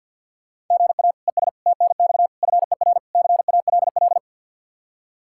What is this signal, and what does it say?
Morse code, heard as a keyed tone, about 51 wpm, 700 Hz. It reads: GA ES TNX FER CALL
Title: Practice QSO - Exchange 1 - Beginning - 50wpm